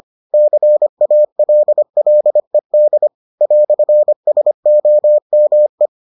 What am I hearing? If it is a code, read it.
Morse code, heard as a keyed tone, about 25 wpm, 600 Hz. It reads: CALLED "SOME